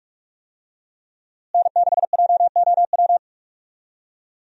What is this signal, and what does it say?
Morse code, heard as a keyed tone, about 42 wpm, 700 Hz. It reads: N6JOW
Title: USA Call Signs - 5 Characters - Repeated 2x - 45wpm